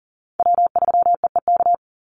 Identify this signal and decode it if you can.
Morse code, heard as a keyed tone, about 40 wpm, 700 Hz. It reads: W3EEK
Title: Call Signs - 40wpm